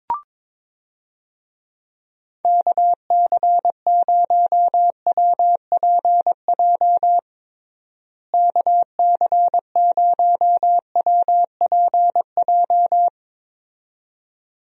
Morse code, heard as a keyed tone, about 22 wpm, 700 Hz. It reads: KC0WPJ KC0WPJ
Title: USA Call Signs - 6 Characters - Repeated 2x - 22wpm